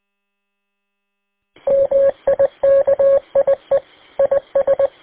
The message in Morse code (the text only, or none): MIKIE IS